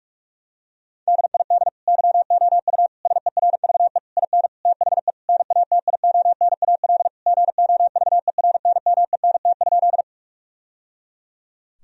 Morse code, heard as a keyed tone, about 46 wpm, 700 Hz. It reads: DID YOU SERVE IN THE NATIONAL GOVERNMENT?
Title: Sentences from Top 1000 Words - Review Part 4 - 45wpm